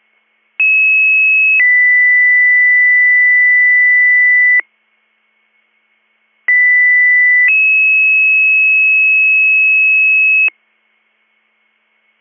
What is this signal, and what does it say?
Two-tone page, heard as a one-second tone, then a three-second tone, about 2600 then 2000 Hz, about 2000 then 2600 Hz. It is heard from 0.6 to 10.5 s.